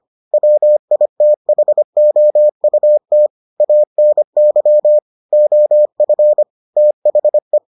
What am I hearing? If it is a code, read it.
Morse code, heard as a keyed tone, about 25 wpm, 600 Hz. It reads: WITHOUT ANY OF THE